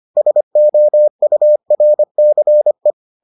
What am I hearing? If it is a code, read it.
Morse code, heard as a keyed tone, about 25 wpm, 600 Hz. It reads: SOURCE